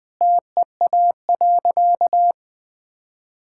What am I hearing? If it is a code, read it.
Morse code, heard as a keyed tone, about 21 wpm, 700 Hz. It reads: TEA.
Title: Sentences with 2-Word Phrases - 20wpm